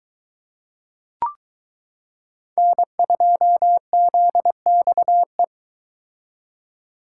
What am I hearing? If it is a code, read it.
Morse code, heard as a keyed tone, about 22 wpm, 700 Hz. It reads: N2ZXE